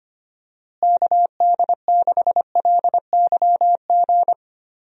Morse code, heard as a keyed tone, about 25 wpm, 700 Hz. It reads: KD6LYG
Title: USA Call Signs - 6 Characters - Repeated 2x - 25wpm